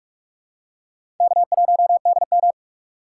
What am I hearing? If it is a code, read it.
Morse code, heard as a keyed tone, about 46 wpm, 700 Hz. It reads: K1DM
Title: K1USN SST QSO Practice - 45wpm